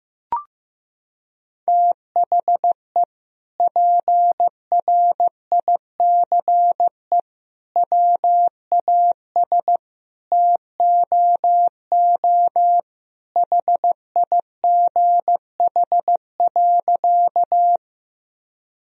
Morse code, heard as a keyed tone, about 15 wpm, 700 Hz. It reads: THE PRICE WAS TOO HIGH.
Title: Sentences from Top 1000 Words - Review Part 4 - 15wpm